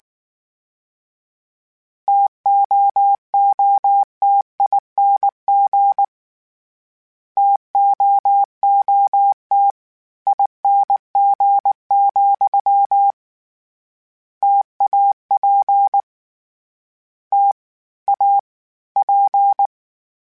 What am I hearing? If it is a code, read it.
Morse code, heard as a keyed tone, about 19 wpm, 800 Hz. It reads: TOOTING TOOT ING, TAP T A P
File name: Variable Spacing / Variable Spacing-mp3s / T AOEs @18